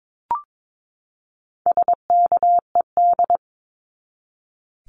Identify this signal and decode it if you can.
Morse code, heard as a keyed tone, about 22 wpm, 700 Hz. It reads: SKED
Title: Abbreviations 22wpm